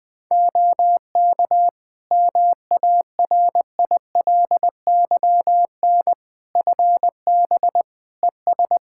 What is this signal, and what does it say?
Morse code, heard as a keyed tone, about 21 wpm, 700 Hz. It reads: OK MARILYN FB ES
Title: Practice QSO - Exchange 2 - Info - 20wpm